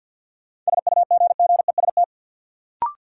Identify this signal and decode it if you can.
Morse code, heard as a keyed tone, about 51 wpm, 700 Hz. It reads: SUGGEST
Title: Sentences from Top 1300 Words - New Word Drill Alt - 50wpm